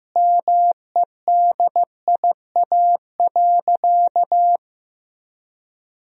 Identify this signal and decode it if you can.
Morse code, heard as a keyed tone, about 16 wpm, 700 Hz. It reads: MEDIA.
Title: Sentences from Top 1000 Words - Review Part 4 - 15wpm